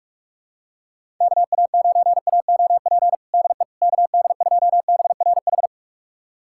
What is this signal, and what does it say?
Morse code, heard as a keyed tone, about 46 wpm, 700 Hz. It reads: KA9AOP DE KD2BRH
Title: USA Callsign DE Callsign - 6 Characters - 45wpm